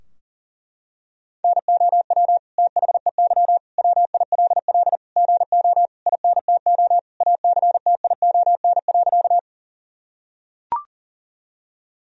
Morse code, heard as a keyed tone, about 40 wpm, 700 Hz. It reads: NOW THEY WILL GO INTO ACTION.
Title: Sentences from Top 1000 Words - Review Part 4 - 40wpm